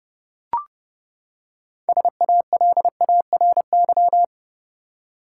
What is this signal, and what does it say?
Morse code, heard as a keyed tone, about 30 wpm, 700 Hz. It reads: SALARY
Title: Six Letter Words - 30wpm